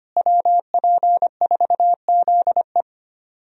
Morse code, heard as a keyed tone, about 25 wpm, 700 Hz. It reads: WP4ZE
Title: USA Call Signs - 5 Characters - Repeated 3x - 25wpm